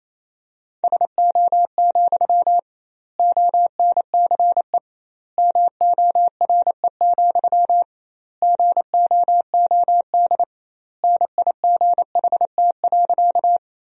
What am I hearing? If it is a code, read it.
Morse code, heard as a keyed tone, about 28 wpm, 700 Hz. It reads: SO, ONCE MORE, GOOD NIGHT.
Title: 2x Word-Spacing - Sentences from Top 300 Words - 28wpm